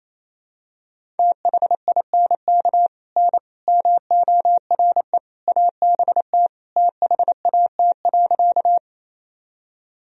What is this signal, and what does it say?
Morse code, heard as a keyed tone, about 28 wpm, 700 Hz. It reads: THINK N MORE ABT THAT.
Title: Sentences from Top 200 Words and Abbreviations 28wpm